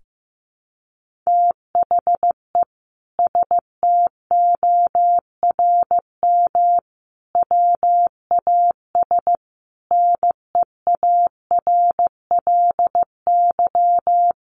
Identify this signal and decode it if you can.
Morse code, heard as a keyed tone, about 15 wpm, 700 Hz. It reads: THE STORM WAS NEARLY
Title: Sentences from Top 1500 Words 15wpm